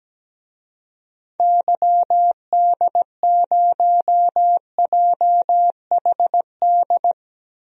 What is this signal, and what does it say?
Morse code, heard as a keyed tone, about 17 wpm, 700 Hz. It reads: YD0JHD
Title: DXCC-100 Call Signs - 6 Characters - Repeated 2x - 17wpm